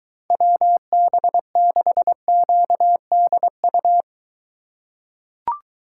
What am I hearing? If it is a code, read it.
Morse code, heard as a keyed tone, about 23 wpm, 700 Hz. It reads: WB6QDU